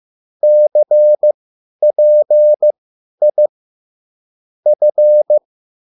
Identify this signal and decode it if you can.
Morse code, heard as a keyed tone, about 15 wpm, spaced as for 10 wpm, 600 Hz. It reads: CPI F
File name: QSL / CW lesson